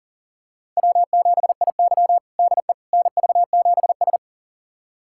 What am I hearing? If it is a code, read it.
Morse code, heard as a keyed tone, about 40 wpm, 700 Hz. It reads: W7IY DE NV7S
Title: USA Callsign DE Callsign - 4 Characters - 40wpm